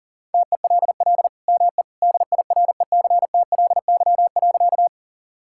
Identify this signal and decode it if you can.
Morse code, heard as a keyed tone, about 37 wpm, 690 Hz. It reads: TELL ME DIRECTLY.
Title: Sentences from Top 1000 Words - Review Part 5 - 40wpm